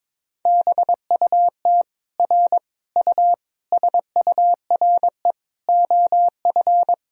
Morse code, heard as a keyed tone, about 22 wpm, 700 Hz. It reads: BUT R U SURE OF
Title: Sentences from Top 300 Words and Abbreviations 22wpm